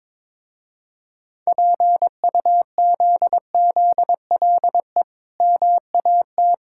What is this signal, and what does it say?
Morse code, heard as a keyed tone, about 23 wpm, 690 Hz. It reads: PUZZLE MAT
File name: Difficult Words with Hints - 22wpm